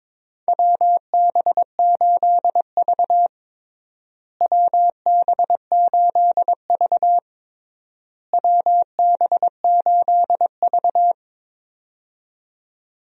Morse code, heard as a keyed tone, about 22 wpm, 700 Hz. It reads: WB8V WB8V WB8V
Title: USA Call Signs - 4 Characters - Repeated 3x - 22wpm